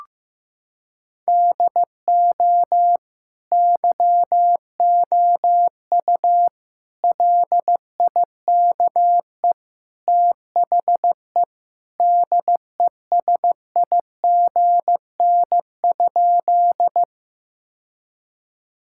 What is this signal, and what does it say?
Morse code, heard as a keyed tone, about 15 wpm, 700 Hz. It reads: DO YOU LIKE THE DESIGN?
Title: Sentences from Top 1000 Words - Review Part 5 - 15wpm